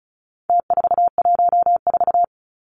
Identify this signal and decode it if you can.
Morse code, heard as a keyed tone, about 34 wpm, 700 Hz. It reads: T414
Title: Numbers - Contest Style - 4-Digit 35wpm